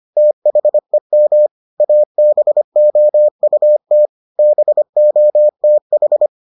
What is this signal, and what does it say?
Morse code, heard as a keyed tone, about 25 wpm, 600 Hz. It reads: THEM ABOUT BOTH